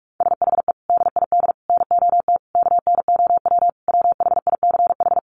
Morse code, heard as a keyed tone, about 44 wpm, 700 Hz. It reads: SHE DID NOT KNOW WHICH
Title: Sentences from Top 1100 Words 45wpm